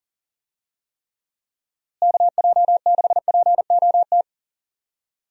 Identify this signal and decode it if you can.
Morse code, heard as a keyed tone, about 39 wpm, 700 Hz. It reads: KJ6POT
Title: USA Call Signs - 6 Characters - Repeated 2x - 40wpm